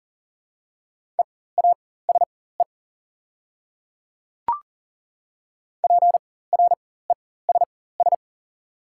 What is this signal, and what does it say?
Morse code, heard as a keyed tone, about 40 wpm, spaced as for 19 wpm, 690 Hz. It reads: EASE PRESS
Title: 4x Character Spacing - Top 500 Words - 40wpm